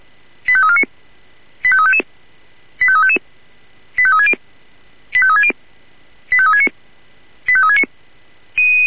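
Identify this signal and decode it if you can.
Five-tone selective call (ZVEI), heard as quick runs of five tones, tones of about 80 ms. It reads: 85349 85355 85355 85360 85360 85369 85369
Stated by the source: ZVEI five-tone call 85340 received by SDR.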